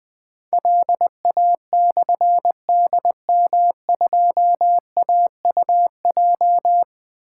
Morse code, heard as a keyed tone, about 20 wpm, 700 Hz. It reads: LA/DM2AUJ
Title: Difficult Call Signs 20wpm